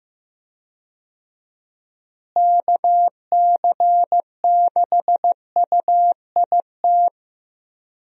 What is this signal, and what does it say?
Morse code, heard as a keyed tone, about 15 wpm, 700 Hz. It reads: KC6UIT
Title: USA Call Signs - 6 Characters - Repeated 2x - 15wpm